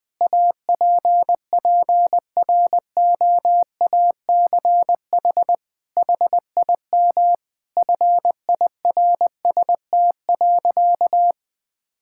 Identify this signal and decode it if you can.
Morse code, heard as a keyed tone, about 20 wpm, 700 Hz. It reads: APPROACH HIM FIRST.
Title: Sentences from Top 1000 Words - Review Part 6 - 20wpm